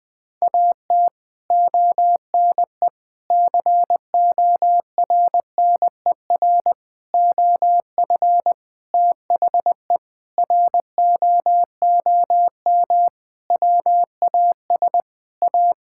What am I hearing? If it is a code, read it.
Morse code, heard as a keyed tone, about 21 wpm, 700 Hz. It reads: AT ONE CORNER OF THE ROOM WAS A
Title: Sentences from Top 1600 Words 20wpm